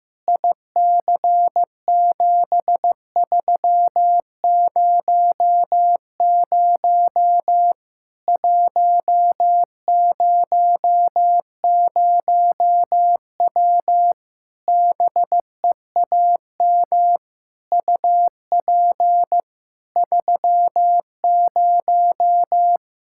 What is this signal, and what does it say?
Morse code, heard as a keyed tone, about 15 wpm, 700 Hz. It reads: IC7300 100W BEAM UP 30
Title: Practice QSO - Exchange 2 - Info - 15wpm